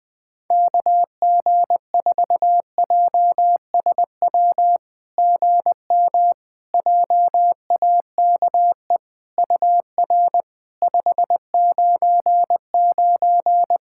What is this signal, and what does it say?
Morse code, heard as a keyed tone, about 21 wpm, 700 Hz. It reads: KG4JSW GM JAKE UR 599
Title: SOTA QSO Practice 20wpm